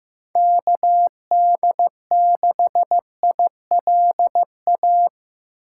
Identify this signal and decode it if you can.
Morse code, heard as a keyed tone, about 18 wpm, 700 Hz. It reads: KD6ILA
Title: Speed Racing - USA 2x3 Call Signs 15wpm